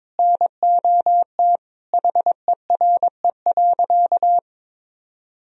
Morse code, heard as a keyed tone, about 22 wpm, 700 Hz. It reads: NOT HERE.
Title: Sentences from Top 1000 Words - Review Part 3 - 22wpm